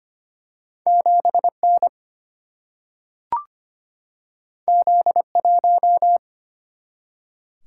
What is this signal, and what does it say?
Morse code, heard as a keyed tone, about 25 wpm, 700 Hz. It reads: 7 N Z 1
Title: Single-Letter Single-Number 25wpm